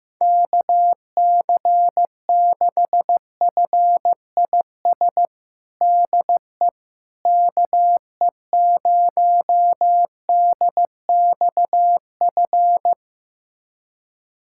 Morse code, heard as a keyed tone, about 15 wpm, 700 Hz. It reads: KC6FIS DE KE0DXF
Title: USA Callsign DE Callsign - 6 Characters - 15wpm